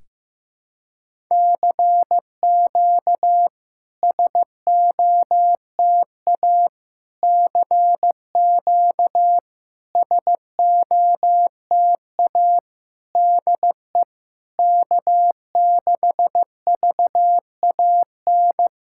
Morse code, heard as a keyed tone, about 15 wpm, 690 Hz. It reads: CQ SOTA CQ SOTA DE K6VAN
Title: SOTA QSO Practice 15wpm